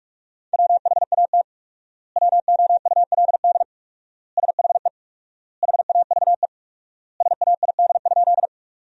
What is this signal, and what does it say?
Morse code, heard as a keyed tone, about 46 wpm, 700 Hz. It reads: WHAT WOULD SHE HAVE SAID?
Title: 4x Word-Spacing - Sentences from Top 100 Words - 45wpm